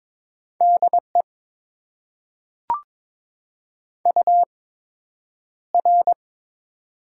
Morse code, heard as a keyed tone, about 22 wpm, spaced as for 9 wpm, 700 Hz. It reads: DE UR
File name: Top 90 QSO Elements - 8x Character Spacing - 22wpm